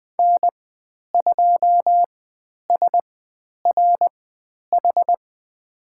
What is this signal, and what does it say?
Morse code, heard as a keyed tone, about 20 wpm, spaced as for 10 wpm, 700 Hz. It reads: N2SRH
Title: 095 Lesson 16g - Letter c - Call Signs 20wpm